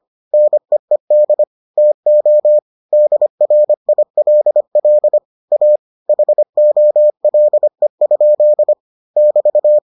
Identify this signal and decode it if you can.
Morse code, heard as a keyed tone, about 25 wpm, 600 Hz. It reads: NEED TO DRILL A HOLE? <BT>